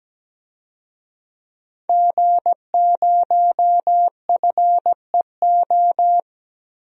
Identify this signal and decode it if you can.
Morse code, heard as a keyed tone, about 17 wpm, 700 Hz. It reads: G0FEO
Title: DXCC-100 Call Signs - 5 Characters - 17wpm